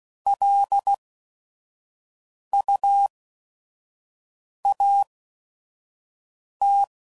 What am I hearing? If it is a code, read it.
Morse code, heard as a keyed tone, about 16 wpm, spaced as for 5 wpm, 800 Hz. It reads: LUAT
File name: CW Code Groups @5WPM / Technical Subjects / AND Gate